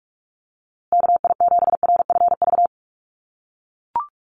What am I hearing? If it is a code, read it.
Morse code, heard as a keyed tone, about 44 wpm, 700 Hz. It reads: KI7RFV
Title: USA Call Signs Repeated 2x 45wpm